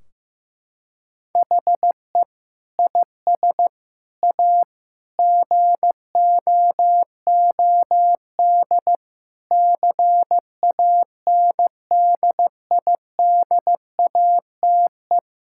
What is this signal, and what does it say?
Morse code, heard as a keyed tone, about 16 wpm, 700 Hz. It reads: HE IS A GOOD CANDIDATE
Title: Sentences from Top 1600 Words 15wpm